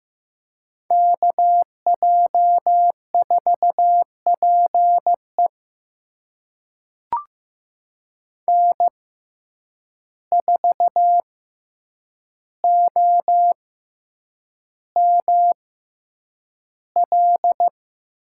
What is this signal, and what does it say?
Morse code, heard as a keyed tone, about 15 wpm, spaced as for 8 wpm, 700 Hz. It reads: KJ4PE N4OML
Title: USA Call Signs - 5 Characters - 6x Character Spacing - 15wpm